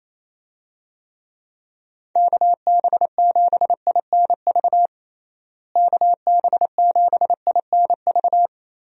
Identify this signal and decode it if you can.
Morse code, heard as a keyed tone, about 28 wpm, 700 Hz. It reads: KB7INV KB7INV
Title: USA Call Signs - 6 Characters - Repeated 2x - 28wpm